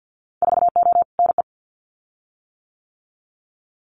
Morse code, heard as a keyed tone, about 50 wpm, 700 Hz. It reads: 4O NE
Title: Winter Field Day Exchanges Encore 50wpm